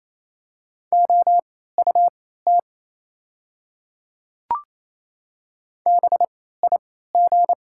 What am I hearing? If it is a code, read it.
Morse code, heard as a keyed tone, about 28 wpm, spaced as for 16 wpm, 700 Hz. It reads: OUT BIG
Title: 3x Character Spacing - Top 200 Words - 28wpm